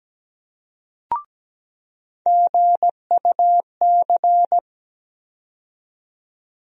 Morse code, heard as a keyed tone, about 17 wpm, 700 Hz. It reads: GUC
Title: USA Call Sign Postfixes - 3 Letters 17wpm